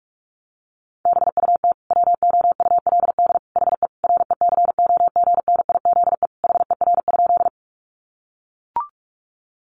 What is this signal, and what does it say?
Morse code, heard as a keyed tone, about 44 wpm, 700 Hz. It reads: BUT WOULD HE RECOGNIZE HER?
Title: Sentences from Top 1400 Words 45wpm